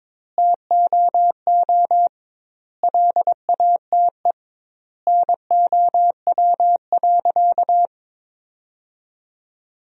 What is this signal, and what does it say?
Morse code, heard as a keyed tone, about 22 wpm, 700 Hz. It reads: TOO LATE NOW.
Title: 2x Word-Spacing - Sentences from Top 300 Words - 22wpm